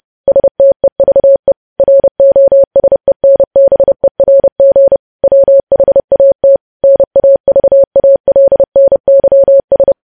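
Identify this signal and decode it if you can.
Morse code, heard as a keyed tone, about 30 wpm, 550 Hz. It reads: STEVE ROSENBERG WHAT NAVALNYS
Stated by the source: Morse code practice of news headlines at 30.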